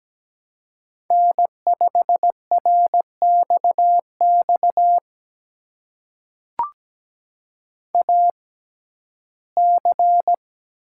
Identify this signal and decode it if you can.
Morse code, heard as a keyed tone, about 17 wpm, spaced as for 12 wpm, 700 Hz. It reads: N5RXX AC
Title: USA Call Signs - 5 Characters - 6x Character Spacing - 17wpm